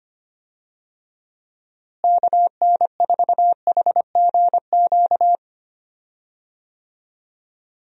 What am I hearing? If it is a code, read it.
Morse code, heard as a keyed tone, about 25 wpm, 700 Hz. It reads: KN4HGQ
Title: USA Call Signs - 6 Characters - Repeated 3x - 25wpm